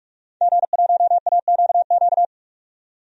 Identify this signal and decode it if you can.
Morse code, heard as a keyed tone, about 46 wpm, 700 Hz. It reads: G1AQQ
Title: DXCC-100 Call Signs - 5 Characters - 45wpm